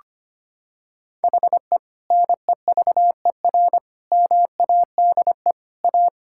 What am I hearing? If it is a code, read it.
Morse code, heard as a keyed tone, about 25 wpm, 700 Hz. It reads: HE NEVER MADE A
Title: Sentences from Top 1600 Words 25wpm